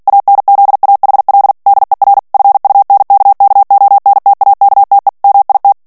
Morse code, heard as a keyed tone, about 48 wpm, 800 Hz. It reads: ANZAHL DER FUNKKONTAKTE MIT
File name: morse mp3 / ALLEMAND texte clair 2 vitesse 45 mots minute